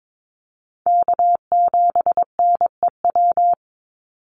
Morse code, heard as a keyed tone, about 22 wpm, 700 Hz. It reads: K7NEW
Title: NA SOTA Chaser Callsigns 22wpm